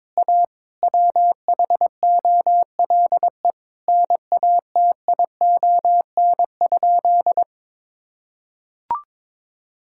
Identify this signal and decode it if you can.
Morse code, heard as a keyed tone, about 22 wpm, 700 Hz. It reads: A WHOLE NATION?
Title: Sentences from Top 1000 Words - Review Part 6 - 22wpm